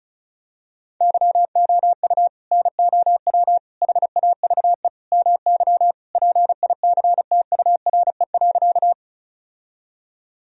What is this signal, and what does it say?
Morse code, heard as a keyed tone, about 34 wpm, 700 Hz. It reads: YOU NOW HAVE MY PICTURE.
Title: Sentences from Top 1000 Words - Review Part 1 - 35wpm